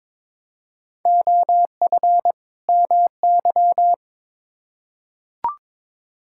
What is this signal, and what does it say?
Morse code, heard as a keyed tone, about 22 wpm, 700 Hz. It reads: OF MY
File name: Build A Phrase - 2-Word Phrases - 22wpm